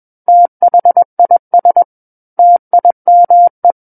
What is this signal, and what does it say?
Morse code, heard as a keyed tone, about 21 wpm, 700 Hz. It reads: THIS TIME
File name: Book One / 12-Chapter 12 - 21 WPM